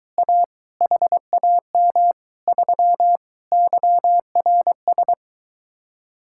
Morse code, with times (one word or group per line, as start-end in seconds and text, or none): 0.2-0.4 A
0.8-2.1 HAM
2.5-3.2 3
3.5-5.1 YRS